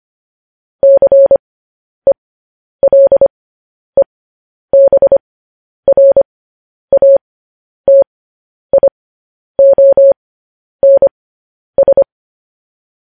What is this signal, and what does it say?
Morse code, heard as a keyed tone, about 25 wpm, spaced as for 10 wpm, 550 Hz. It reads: CELEBRATIONS
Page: News Headlines in Morse Code at 10 WPM